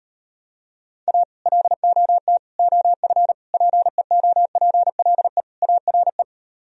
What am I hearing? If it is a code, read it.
Morse code, heard as a keyed tone, about 38 wpm, 700 Hz. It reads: A LOT OF PEOPLE ARE